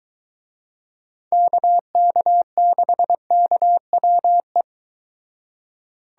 Morse code, heard as a keyed tone, about 22 wpm, 700 Hz. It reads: KK6KWE